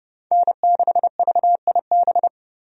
Morse code, heard as a keyed tone, about 31 wpm, 700 Hz. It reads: N6VIB
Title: USA Call Signs - 5 Characters - 30wpm